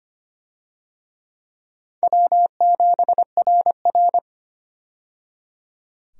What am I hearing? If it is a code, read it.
Morse code, heard as a keyed tone, about 25 wpm, 700 Hz. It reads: W7RR
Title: USA Call Signs - 4 Characters - 25wpm